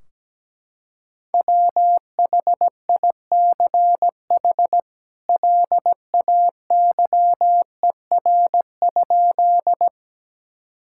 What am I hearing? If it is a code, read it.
Morse code, heard as a keyed tone, about 17 wpm, 700 Hz. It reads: WHICH LAYER?
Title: Sentences from Top 1600 Words 17wpm